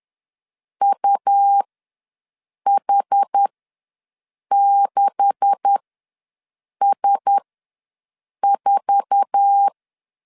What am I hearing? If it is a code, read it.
Morse code, heard as a keyed tone, about 10 wpm, spaced as for 6 wpm, 800 Hz. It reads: UH6S4
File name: LEZIONI / 9 286 ws 10wpm 4032